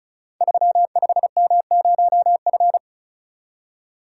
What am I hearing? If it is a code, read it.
Morse code, heard as a keyed tone, about 36 wpm, 690 Hz. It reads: 35M0F
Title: Code Groups - 5 Letter:Numbers - 35wpm